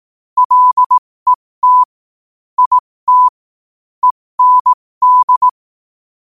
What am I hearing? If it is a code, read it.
Morse code, heard as a keyed tone, about 18 wpm, spaced as for 15 wpm, 1000 Hz. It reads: LET IT END